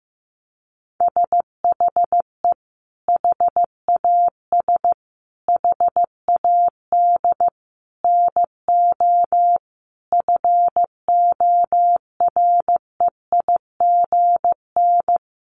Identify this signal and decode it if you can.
Morse code, heard as a keyed tone, about 15 wpm, 700 Hz. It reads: SHE HAS HAD NO FOREIGN
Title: Sentences from Top 1400 Words 15wpm